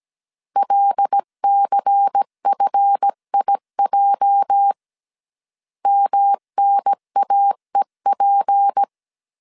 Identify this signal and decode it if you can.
Morse code, heard as a keyed tone, about 17 wpm, 800 Hz. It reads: LCFIJ MNAEP
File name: 1500C S B 16wpm